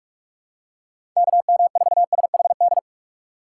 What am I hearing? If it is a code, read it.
Morse code, heard as a keyed tone, about 46 wpm, 700 Hz. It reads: KM4SHD
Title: USA Call Signs - 3x Character Spacing - 45wpm